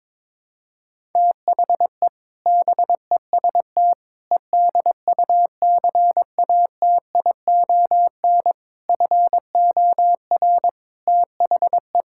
Morse code, heard as a keyed tone, about 22 wpm, 700 Hz. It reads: THE BEST EDUCATION FOR THE
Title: Sentences from Top 1000 Words - Review Part 6 - 22wpm